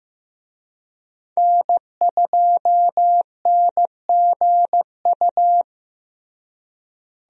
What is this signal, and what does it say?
Morse code, heard as a keyed tone, about 15 wpm, 700 Hz. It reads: N2NGU
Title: USA Call Signs - 5 Characters - Repeated 3x - 15wpm